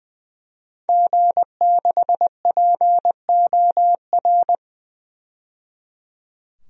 Morse code, heard as a keyed tone, about 20 wpm, 700 Hz. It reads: G6POR
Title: DXCC-100 Call Signs - 5 Characters - 20wpm